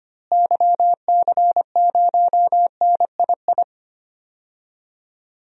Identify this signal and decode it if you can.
Morse code, heard as a keyed tone, about 25 wpm, 700 Hz. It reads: YC0NII
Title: DXCC-100 Call Signs - 6 Characters - Repeated 2x - 25wpm